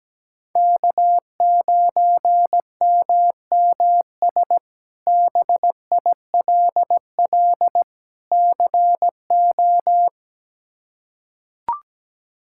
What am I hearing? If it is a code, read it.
Morse code, heard as a keyed tone, about 17 wpm, 700 Hz. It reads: K9MMS BILL CO
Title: K1USN SST QSO Practice - 17wpm